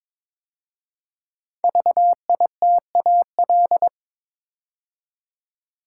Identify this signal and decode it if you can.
Morse code, heard as a keyed tone, about 22 wpm, 700 Hz. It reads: VITAL
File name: 5 Letter Words - First 3 Letters Given - 22wpm